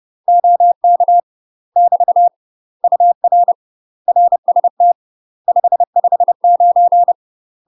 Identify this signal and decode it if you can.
Morse code, heard as a keyed tone, about 30 wpm, 700 Hz. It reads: OK <BT> UR RST 559
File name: UK MASTER QSOs TRAINING / 30WPM / UK MASTER QSOs 0014